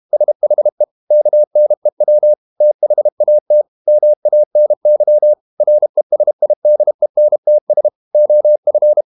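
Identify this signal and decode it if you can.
Morse code, heard as a keyed tone, about 32 wpm, 600 Hz. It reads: SHE KNEW THAT MANY RESIDENTS OF
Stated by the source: Morse code practice at 32 wpm from CW med Gnister, reading Ken Follett's novel Never in English.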